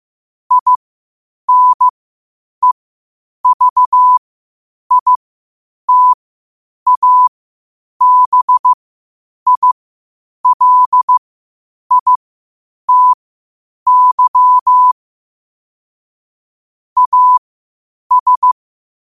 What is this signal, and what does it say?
Morse code, heard as a keyed tone, about 15 wpm, spaced as for 8 wpm, 1000 Hz. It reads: INEVITABILITY AS
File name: qotdfiles / 8wpm